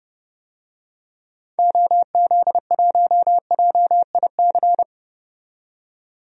Morse code, heard as a keyed tone, about 30 wpm, 700 Hz. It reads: OZ1JIC
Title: DXCC-100 Call Signs - 6 Characters - 30wpm